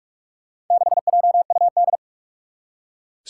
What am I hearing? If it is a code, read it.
Morse code, heard as a keyed tone, about 44 wpm, 700 Hz. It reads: 6JUD
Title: Four Challenging Letter-Number Combinations 45wpm